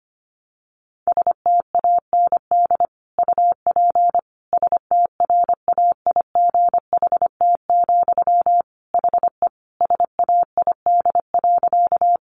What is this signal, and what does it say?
Morse code, heard as a keyed tone, about 25 wpm, 700 Hz. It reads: STAND UP STRAIGHT, HE SAID.
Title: Sentences from Top 1000 Words 25wpm